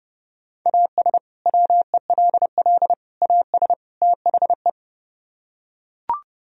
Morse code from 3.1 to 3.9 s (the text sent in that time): AS